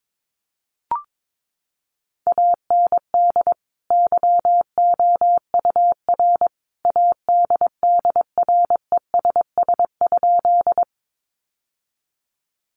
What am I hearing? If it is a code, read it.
Morse code, heard as a keyed tone, about 22 wpm, 700 Hz. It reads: AND YOUR ADDRESS?
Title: Sentences from Top 1000 Words 22wpm